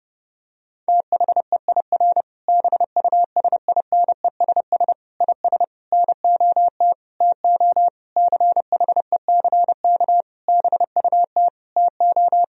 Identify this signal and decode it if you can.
Morse code, heard as a keyed tone, about 30 wpm, 700 Hz. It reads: THEIR BUSINESS IS NOT TO CHECK BUT TO
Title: Sentences from Top 1000 Words - Review Part 3 - 30wpm